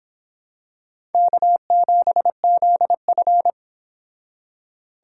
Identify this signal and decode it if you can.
Morse code, heard as a keyed tone, about 25 wpm, 700 Hz. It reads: K7ZF